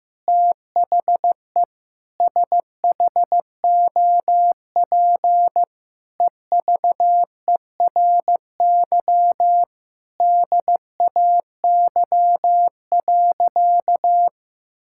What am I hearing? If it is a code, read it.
Morse code, heard as a keyed tone, about 15 wpm, 700 Hz. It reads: THE SHOP EVERY DAY.
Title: Sentences from Top 1600 Words 15wpm